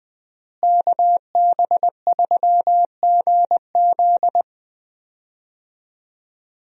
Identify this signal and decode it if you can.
Morse code, heard as a keyed tone, about 21 wpm, 700 Hz. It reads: KB3GZ
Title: USA Call Signs - 5 Characters - Repeated 2x - 20wpm